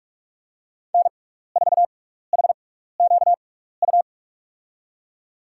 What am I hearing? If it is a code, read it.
Morse code, heard as a keyed tone, about 43 wpm, spaced as for 15 wpm, 700 Hz. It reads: N4HQU
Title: USA Call Signs - 5 Characters - 6x Character Spacing - 45wpm